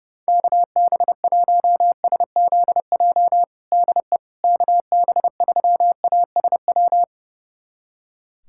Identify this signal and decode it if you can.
Morse code, heard as a keyed tone, about 30 wpm, 700 Hz. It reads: KB1SZJ DE KB3ASW
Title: USA Callsign DE Callsign - 6 Characters - 30wpm